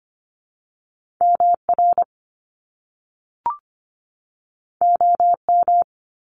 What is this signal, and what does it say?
Morse code, heard as a keyed tone, about 25 wpm, 700 Hz. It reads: MR OM
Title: Top Two Letter Combinations in Words 25wpm